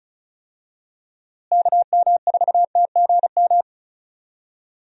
Morse code, heard as a keyed tone, about 36 wpm, 700 Hz. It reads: KM4TGM
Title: USA Call Signs - 6 Characters - Repeated 2x - 35wpm